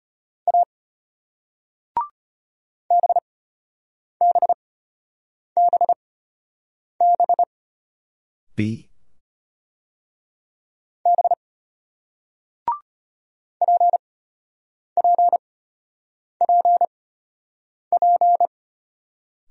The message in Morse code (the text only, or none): A BBBB B PPPP